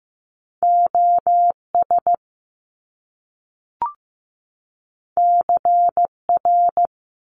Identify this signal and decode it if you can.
Morse code, heard as a keyed tone, about 15 wpm, 700 Hz. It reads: OS CR
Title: Top Two Letter Combinations in Words 15wpm